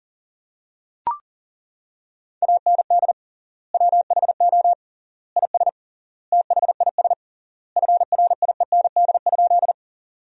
Morse code, heard as a keyed tone, about 40 wpm, 700 Hz. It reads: AND WHO IS THIS FRIEND?
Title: 3x Word-Spacing - Sentences from Top 300 Words - 40wpm